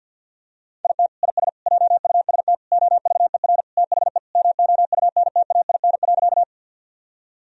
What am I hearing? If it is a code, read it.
Morse code, heard as a keyed tone, about 52 wpm, 700 Hz. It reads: IT IS JUST OVER THE MOUNTAIN.
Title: Sentences from Top 1000 Words - Review Part 1 - 50wpm